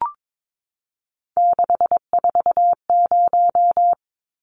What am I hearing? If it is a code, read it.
Morse code, heard as a keyed tone, about 22 wpm, 700 Hz. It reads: 640
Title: Numbers - 3-Digit 22wpm